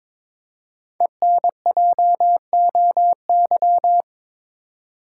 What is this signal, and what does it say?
Morse code, heard as a keyed tone, about 22 wpm, 700 Hz. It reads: ENJOY
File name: 5 Letter Words - First 3 Letters Given - 22wpm